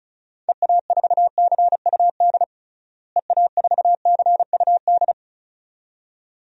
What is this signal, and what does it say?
Morse code, heard as a keyed tone, about 34 wpm, 700 Hz. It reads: EA4CUD EA4CUD
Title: DXCC-100 Call Signs - 6 Characters - Repeated 2x - 35wpm